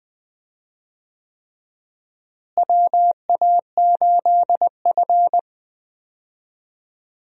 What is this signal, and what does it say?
Morse code, heard as a keyed tone, about 20 wpm, 700 Hz. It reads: WA8F
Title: USA Call Signs - 4 Characters - Repeated 3x - 20wpm